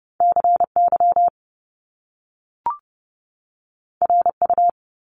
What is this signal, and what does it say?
Morse code, heard as a keyed tone, about 30 wpm, 700 Hz. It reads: CY RU
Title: Top Two Letter Combinations in Words 30wpm